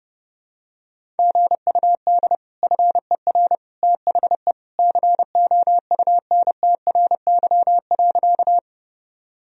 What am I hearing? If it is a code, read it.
Morse code, heard as a keyed tone, about 32 wpm, 700 Hz. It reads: GUD FER THE COUNTRY.
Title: Sentences from Top 300 Words and Abbreviations 30wpm